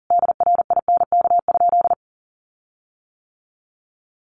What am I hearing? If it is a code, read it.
Morse code, heard as a keyed tone, about 40 wpm, 700 Hz. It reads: DRINK?
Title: Sentences from Top 1500 Words 40wpm